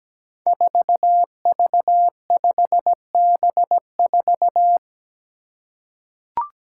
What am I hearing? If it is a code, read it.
Morse code, heard as a keyed tone, about 17 wpm, 700 Hz. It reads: 4V5B4
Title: Five Challenging Letter-Number Combinations 17wpm